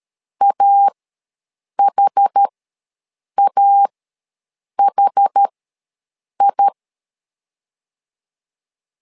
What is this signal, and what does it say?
Morse code, heard as a keyed tone, about 13 wpm, spaced as for 7 wpm, 800 Hz. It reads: AHAHI
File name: LEZIONI / 2 UHDN / 2 UH ws 12wpm 1730